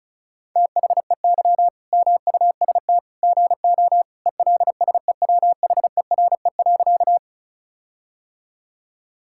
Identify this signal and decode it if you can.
Morse code, heard as a keyed tone, about 36 wpm, 700 Hz. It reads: THEY MUST GO ELSEWHERE.
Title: Sentences from Top 1600 Words 35wpm